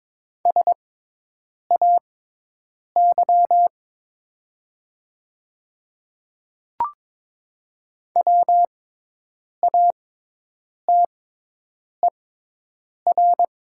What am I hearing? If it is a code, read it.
Morse code, heard as a keyed tone, about 22 wpm, spaced as for 8 wpm, 700 Hz. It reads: SAY WATER
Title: 6x Character Spacing - Top 200 Words - 22wpm